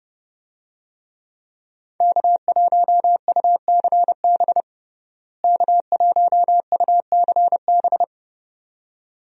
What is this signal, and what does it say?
Morse code, heard as a keyed tone, about 32 wpm, 700 Hz. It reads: K1UCB K1UCB
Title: USA Call Signs - 5 Characters - Repeated 2x - 30wpm